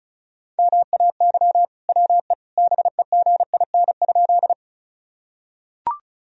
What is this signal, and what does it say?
Morse code, heard as a keyed tone, about 34 wpm, 700 Hz. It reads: MAY WE BEGIN?
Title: Sentences from Top 1000 Words - Review Part 1 - 35wpm